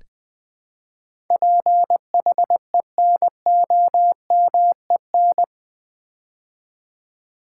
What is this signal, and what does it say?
Morse code, heard as a keyed tone, about 20 wpm, 700 Hz. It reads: PHENOMEN
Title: Build A Word - 4 Letters At A Time - 10-Letter Words - 20wpm